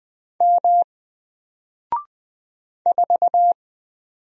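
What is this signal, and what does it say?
Morse code, heard as a keyed tone, about 20 wpm, 700 Hz. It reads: M 4
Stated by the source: Single letter and number Morse drill at 20 wpm, from N4LCD.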